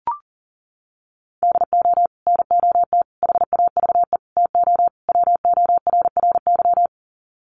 Morse code, heard as a keyed tone, about 40 wpm, 700 Hz. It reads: DO NOT HAVE TO WORRY
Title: Sets of 5 Words 40wpm